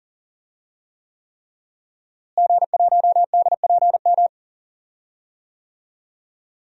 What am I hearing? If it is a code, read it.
Morse code, heard as a keyed tone, about 40 wpm, 700 Hz. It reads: G1DPM
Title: DXCC-100 Call Signs - 5 Characters - Repeated 3x - 40wpm